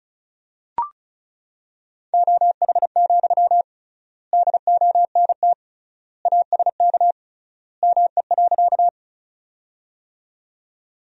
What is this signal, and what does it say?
Morse code, heard as a keyed tone, about 34 wpm, 700 Hz. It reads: OH, DONT ASK ME.
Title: 3x Word-Spacing - Sentences from Top 500 Words - 35wpm